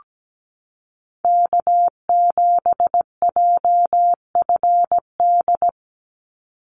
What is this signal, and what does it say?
Morse code, heard as a keyed tone, about 17 wpm, 700 Hz. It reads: K7JFD
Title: NA SOTA Chaser Callsigns 17wpm